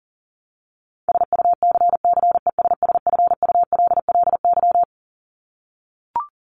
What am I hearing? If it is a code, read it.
Morse code, heard as a keyed tone, about 40 wpm, 700 Hz. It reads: SUCCESSFULLY
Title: LY Suffix - Top 100 Words with Base Words 40wpm